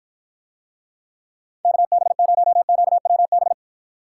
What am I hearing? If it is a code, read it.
Morse code, heard as a keyed tone, about 53 wpm, 700 Hz. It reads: KB0QWB